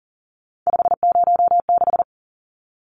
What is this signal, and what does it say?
Morse code, heard as a keyed tone, about 40 wpm, 700 Hz. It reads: 506
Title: Numbers - 3-Digit 40wpm